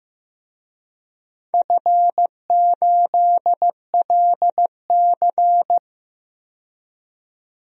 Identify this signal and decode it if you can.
Morse code, heard as a keyed tone, about 15 wpm, 700 Hz. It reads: F8LC
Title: DXCC-100 Call Signs - 4 Characters - 15wpm